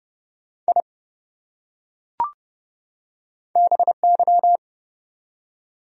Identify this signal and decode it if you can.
Morse code, heard as a keyed tone, about 30 wpm, 700 Hz. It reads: I BY
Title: Top 100 Words 30wpm